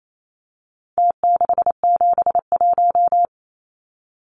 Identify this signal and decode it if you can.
Morse code, heard as a keyed tone, about 28 wpm, 700 Hz. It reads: T671
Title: Numbers - Contest Style - 4-Digit 28wpm